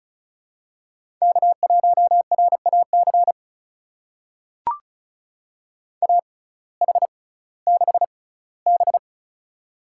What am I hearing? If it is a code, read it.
Morse code, heard as a keyed tone, about 36 wpm, spaced as for 19 wpm, 700 Hz. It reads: K1RAC AH6B
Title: USA Call Signs - 6x Character Spacing - 35wpm